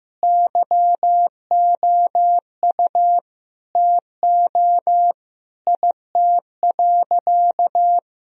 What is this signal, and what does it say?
Morse code, heard as a keyed tone, about 15 wpm, 700 Hz. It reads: YOU TO IT.
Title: Sentences from Top 1000 Words - Review Part 2 - 15wpm